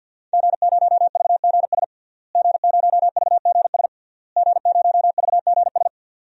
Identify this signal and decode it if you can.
Morse code, heard as a keyed tone, about 46 wpm, 700 Hz. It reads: G0VGS G0VGS G0VGS
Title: DXCC-100 Call Signs - 5 Characters - Repeated 3x - 50wpm